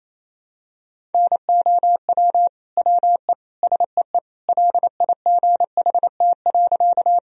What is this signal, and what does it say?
Morse code, heard as a keyed tone, about 28 wpm, 700 Hz. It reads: NOW WE SEE LIGHT.
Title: Sentences from Top 1000 Words - Review Part 1 - 28wpm